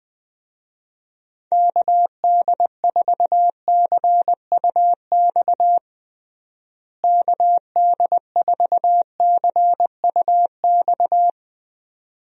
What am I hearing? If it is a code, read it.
Morse code, heard as a keyed tone, about 20 wpm, 700 Hz. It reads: KD4CUX KD4CUX
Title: USA Call Signs - 6 Characters - Repeated 2x - 20wpm